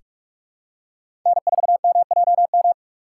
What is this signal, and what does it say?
Morse code, heard as a keyed tone, about 43 wpm, 700 Hz. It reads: N4MJM
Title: USA Call Signs - 2x Character Spacing - 45wpm